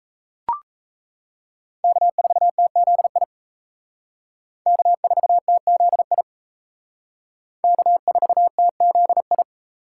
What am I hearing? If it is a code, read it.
Morse code, heard as a keyed tone, about 35 wpm, 700 Hz. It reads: K4TZI K4TZI K4TZI
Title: Speed Racing - USA 1x3 Call Signs 28wpm